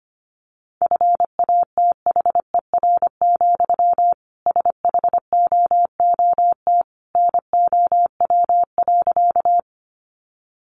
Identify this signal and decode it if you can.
Morse code, heard as a keyed tone, about 25 wpm, 700 Hz. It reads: FATHER, SHOOT NOW.
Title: Sentences from Top 1200 Words 25wpm